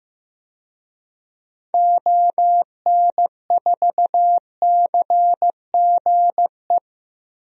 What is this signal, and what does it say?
Morse code, heard as a keyed tone, about 16 wpm, 700 Hz. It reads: ON4CGE
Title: DXCC-100 Call Signs - 6 Characters - Repeated 2x - 15wpm